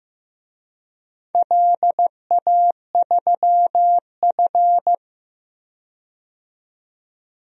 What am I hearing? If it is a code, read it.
Morse code, heard as a keyed tone, about 15 wpm, 700 Hz. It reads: LA3F
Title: DXCC-100 Call Signs - 4 Characters - 15wpm